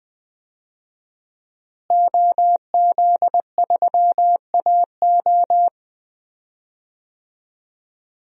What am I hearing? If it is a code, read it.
Morse code, heard as a keyed tone, about 21 wpm, 700 Hz. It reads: OZ3AO
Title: DXCC-100 Call Signs - 5 Characters - Repeated 2x - 20wpm